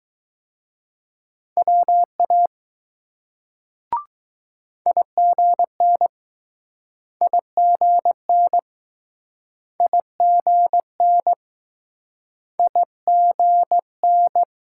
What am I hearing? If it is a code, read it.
Morse code, heard as a keyed tone, about 20 wpm, 700 Hz. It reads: WA IGN IGN IGN IGN